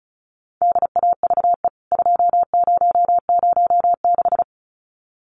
Morse code, heard as a keyed tone, about 34 wpm, 700 Hz. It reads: DAVE 2006
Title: CWT Exchange 35wpm